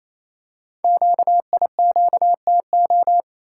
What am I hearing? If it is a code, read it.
Morse code, heard as a keyed tone, about 29 wpm, 700 Hz. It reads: QIQTO
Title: Code Groups - 5 Letters - 28wpm